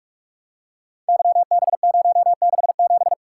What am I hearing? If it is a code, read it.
Morse code, heard as a keyed tone, about 44 wpm, 700 Hz. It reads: YB067
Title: Code Groups - 5 Letter:Numbers - 45wpm